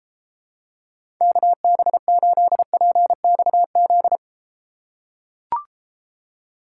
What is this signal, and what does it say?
Morse code, heard as a keyed tone, about 33 wpm, 700 Hz. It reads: KB8PXZ